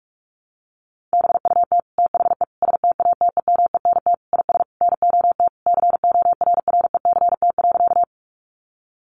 Morse code, heard as a keyed tone, about 44 wpm, 700 Hz. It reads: BUT THE STATEMENT IS NOT CORRECT.
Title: Sentences from Top 1200 Words 45wpm